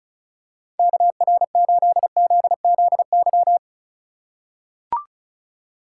Morse code, heard as a keyed tone, about 34 wpm, 700 Hz. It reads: KR8ZZY
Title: USA Call Signs - 6 Characters - 4x Character Spacing - 35wpm